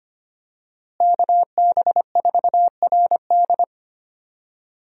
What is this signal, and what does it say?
Morse code, heard as a keyed tone, about 25 wpm, 700 Hz. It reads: KB4RD
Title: USA Call Signs - 5 Characters - Repeated 2x - 25wpm